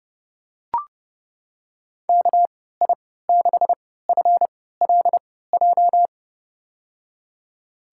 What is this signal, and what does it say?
Morse code, heard as a keyed tone, about 30 wpm, spaced as for 17 wpm, 700 Hz. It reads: KI6FLJ
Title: USA Call Signs - 6 Characters - 3x Character Spacing - 30wpm